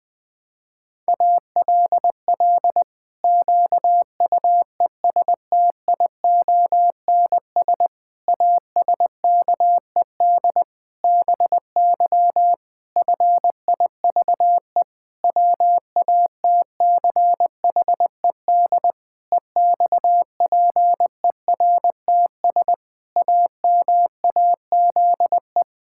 Morse code, heard as a keyed tone, about 20 wpm, 700 Hz. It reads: ALL QUESTIONS ASKED BY FIVE WATCHED EXPERTS AMAZE
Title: Pangram Challenge v2 - 20wpm